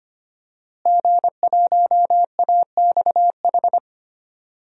Morse code, heard as a keyed tone, about 25 wpm, 700 Hz. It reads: G1AXH
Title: DXCC-100 Call Signs - 5 Characters - Repeated 2x - 25wpm